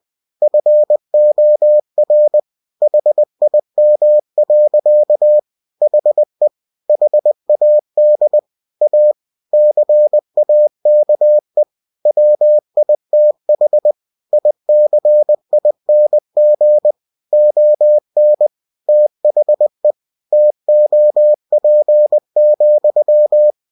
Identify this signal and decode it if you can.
Morse code, heard as a keyed tone, about 20 wpm, 600 Hz. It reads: FOR HIM. HE HAD A CAKE WITH ICING ON THE TOP,